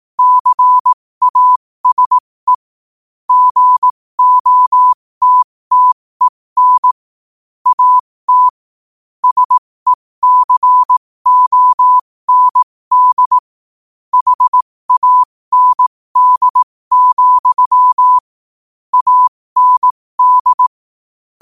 Morse code, heard as a keyed tone, about 18 wpm, spaced as for 15 wpm, 1000 Hz. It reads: CASE GOTTEN AT SECOND HAND, AND
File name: qotdfiles / 15wpm